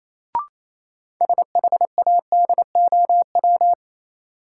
Morse code, heard as a keyed tone, about 28 wpm, 700 Hz. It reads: SHADOW
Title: Sets of 5 Words Encore - New Word Drill - 28wpm